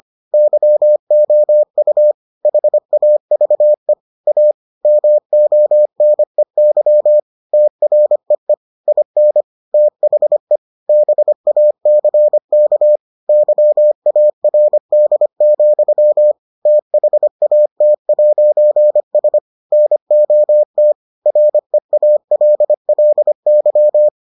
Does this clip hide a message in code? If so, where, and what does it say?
Yes: Morse code from 0.3 to 24.1 s, heard as a keyed tone, about 25 wpm, 600 Hz. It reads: YOU HAVE A MONEY TREE IN THE BACK YARD, THAT'S NOT REALLY